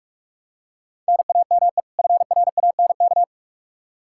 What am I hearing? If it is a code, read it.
Morse code, heard as a keyed tone, about 42 wpm, 700 Hz. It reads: NAME FRANK
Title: K1USN SST QSO Practice 45wpm